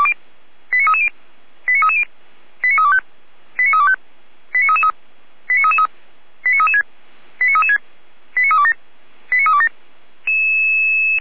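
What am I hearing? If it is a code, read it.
Five-tone selective call (ZVEI), heard as quick runs of five tones, tones of about 70 ms. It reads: 89330 89330 89325 89325 89333 89333 89336 89336 89327 89327
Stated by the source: ZVEI five-tone selective call received on SDR, server of feuerwehr-orlamuende (fire brigade Orlamünde).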